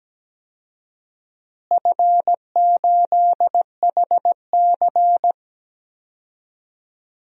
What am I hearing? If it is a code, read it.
Morse code, heard as a keyed tone, about 17 wpm, 700 Hz. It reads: F8HC
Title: DXCC-100 Call Signs - 4 Characters - Repeated 2x - 17wpm